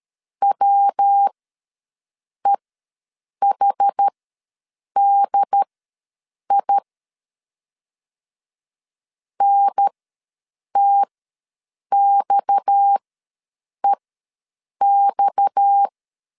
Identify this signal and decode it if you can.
Morse code, heard as a keyed tone, about 13 wpm, spaced as for 7 wpm, 800 Hz. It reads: WEHDI NTXEX
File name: LEZIONI / 6 PKX0 / 6 PKX ws 12wpm 3106